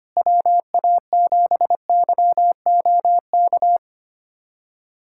Morse code, heard as a keyed tone, about 25 wpm, 700 Hz. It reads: WA7YOK
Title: USA Callsign DE Callsign - 6 Characters - 25wpm